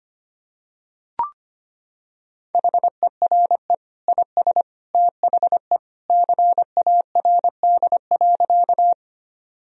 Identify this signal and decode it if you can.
Morse code, heard as a keyed tone, about 25 wpm, 700 Hz. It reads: HERE IS THE CARD.
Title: Sentences from Top 1000 Words - Review Part 4 - 25wpm